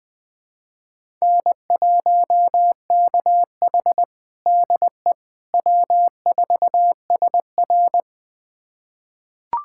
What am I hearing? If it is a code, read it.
Morse code, heard as a keyed tone, about 21 wpm, 700 Hz. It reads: N1KH DE W4SR
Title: USA Callsign DE Callsign - 4 Characters - 20wpm